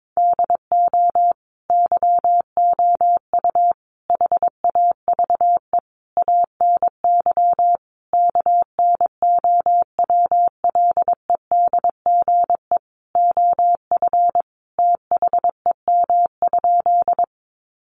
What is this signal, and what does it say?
Morse code, heard as a keyed tone, about 22 wpm, 700 Hz. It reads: DO YOU HAVE ANY KNOWLEDGE OF THEM?
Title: Sentences from Top 600 Words 22wpm